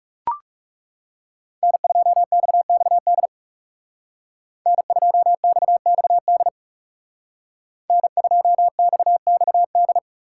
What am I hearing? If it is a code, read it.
Morse code, heard as a keyed tone, about 39 wpm, 700 Hz. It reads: N2XXD N2XXD N2XXD